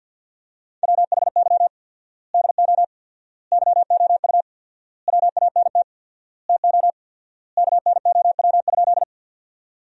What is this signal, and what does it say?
Morse code, heard as a keyed tone, about 50 wpm, 700 Hz. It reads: WHY DO YOU WANT TO KNOW?
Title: 4x Word-Spacing - Sentences from Top 200 Words - 50wpm